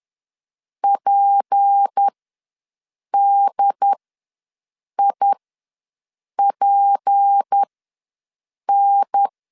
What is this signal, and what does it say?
Morse code, heard as a keyed tone, about 10 wpm, spaced as for 6 wpm, 800 Hz. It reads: PDIPN
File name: LEZIONI / 6 PKX0 / 6 P ws 10wpm 3751